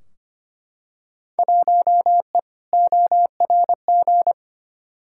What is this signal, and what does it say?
Morse code, heard as a keyed tone, about 25 wpm, 700 Hz. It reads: 1E ORG
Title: Field Day Exchanges - 25wpm